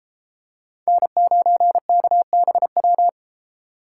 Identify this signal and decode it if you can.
Morse code, heard as a keyed tone, about 31 wpm, 700 Hz. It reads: N9KBW